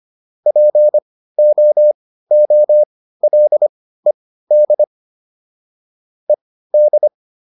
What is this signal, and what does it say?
Morse code, heard as a keyed tone, about 25 wpm, spaced as for 15 wpm, 600 Hz. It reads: POOLED ED